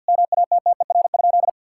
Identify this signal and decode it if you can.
Morse code, heard as a keyed tone, about 52 wpm, 700 Hz. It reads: MATTER?
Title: Sentences from Top 1000 Words - Review Part 6 - 50wpm